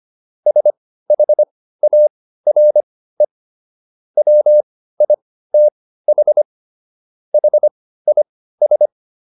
Morse code, heard as a keyed tone, about 25 wpm, spaced as for 15 wpm, 600 Hz. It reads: SHARE WITH HIS